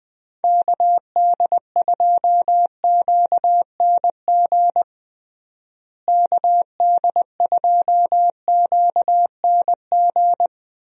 Morse code, heard as a keyed tone, about 21 wpm, 700 Hz. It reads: KD2QNG KD2QNG
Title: USA Call Signs - 6 Characters - Repeated 3x - 20wpm